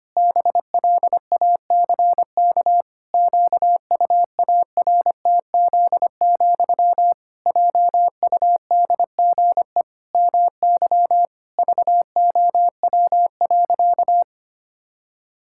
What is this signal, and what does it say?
Morse code, heard as a keyed tone, about 25 wpm, 700 Hz. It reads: BLACK QUARTZ, JUDGE MY VOW.
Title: Pangram Challenge v2 - 25wpm